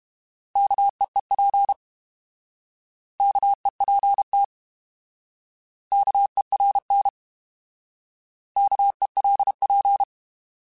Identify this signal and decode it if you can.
Morse code, heard as a keyed tone, about 32 wpm, 800 Hz. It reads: KEEP KEPT KERN KELP
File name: Alphabetical / Alphabetical mp3s / Ks @30